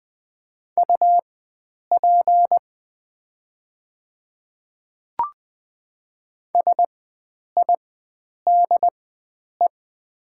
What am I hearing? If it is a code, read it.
Morse code, heard as a keyed tone, about 20 wpm, spaced as for 9 wpm, 700 Hz. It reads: UP SIDE
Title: 4x Character Spacing - Top 100 Words - 20wpm